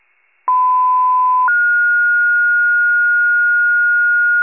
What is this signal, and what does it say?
Two-tone page, heard as a one-second tone, then a three-second tone, about 1010 then 1430 Hz. It is heard from 0.5 to 4.4 s.